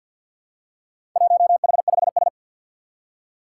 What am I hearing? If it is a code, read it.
Morse code, heard as a keyed tone, about 51 wpm, 700 Hz. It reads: 1H5S
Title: Four Challenging Letter-Number Combinations 50wpm